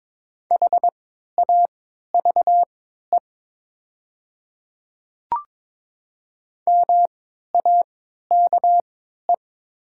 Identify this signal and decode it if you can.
Morse code, heard as a keyed tone, about 22 wpm, spaced as for 13 wpm, 700 Hz. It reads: HAVE MAKE
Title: 3x Character Spacing - Top 300 Words - 22wpm